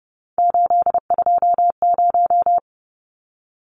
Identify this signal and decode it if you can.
Morse code, heard as a keyed tone, about 30 wpm, 700 Hz. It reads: 820
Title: Numbers - 3-Digit 30wpm